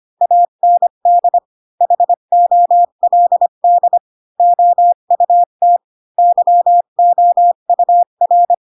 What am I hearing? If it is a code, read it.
Morse code, heard as a keyed tone, about 25 wpm, 700 Hz. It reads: AND HOLD OUT YOUR